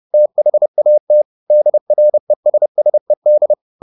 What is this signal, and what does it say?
Morse code, heard as a keyed tone, about 30 wpm, 600 Hz. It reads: THAT DRESSED
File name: the old man and the sea / the old man and the sea 1 30wpm